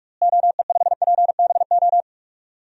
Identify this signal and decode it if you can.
Morse code, heard as a keyed tone, about 47 wpm, 700 Hz. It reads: OE5PBO
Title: DXCC-100 Call Signs - 6 Characters - Repeated 2x - 45wpm